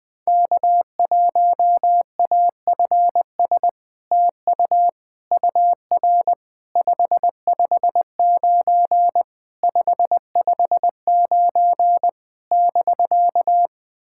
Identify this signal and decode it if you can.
Morse code, heard as a keyed tone, about 20 wpm, 690 Hz. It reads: K1AFS TU UR 559 559 <BK>
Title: SOTA QSO Practice 20wpm